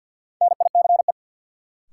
Morse code, heard as a keyed tone, about 50 wpm, 700 Hz. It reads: NICE
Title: Sentences from Top 1100 Words - New Word Drill Alt - 50wpm